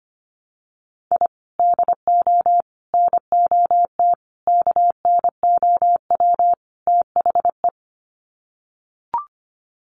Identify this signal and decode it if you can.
Morse code, heard as a keyed tone, about 25 wpm, 700 Hz. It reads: I DO NOT KNOW THE
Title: Sets of 5 Words 25wpm